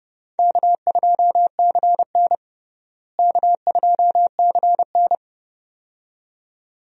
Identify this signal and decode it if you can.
Morse code, heard as a keyed tone, about 30 wpm, 700 Hz. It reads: K2CN K2CN
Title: USA Call Signs - 4 Characters - Repeated 2x - 30wpm